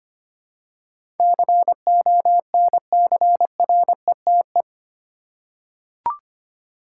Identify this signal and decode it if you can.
Morse code, heard as a keyed tone, about 25 wpm, 700 Hz. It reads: CONCRETE
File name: Eight Letter Words - 25wpm